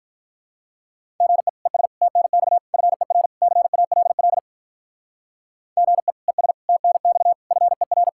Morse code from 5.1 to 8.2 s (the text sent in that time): GE ES TNX FER